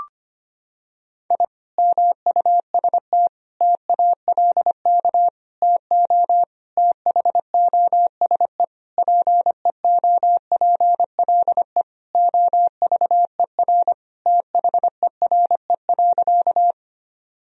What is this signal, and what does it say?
Morse code, heard as a keyed tone, about 25 wpm, 700 Hz. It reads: I MUST TALK TO THOSE PEOPLE OVER THERE.
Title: Sentences from Top 1000 Words - Review Part 1 - 25wpm